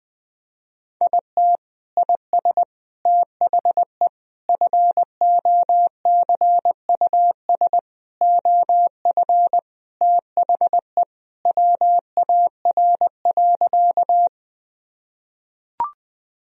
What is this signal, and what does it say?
Morse code, heard as a keyed tone, about 20 wpm, 700 Hz. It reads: IT IS THE FOCUS OF THE WAR.
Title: Sentences from Top 1000 Words - Review Part 4 - 20wpm